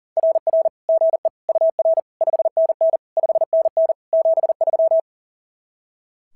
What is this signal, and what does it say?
Morse code, heard as a keyed tone, about 36 wpm, 660 Hz. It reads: RR GE UR 5NN 5NN 73
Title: SOTA QSO Practice 40wpm